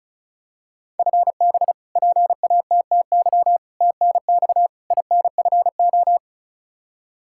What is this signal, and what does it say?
Morse code, heard as a keyed tone, about 36 wpm, 700 Hz. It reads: FB PATTY TNX INFO
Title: K1USN SST QSO Practice 35wpm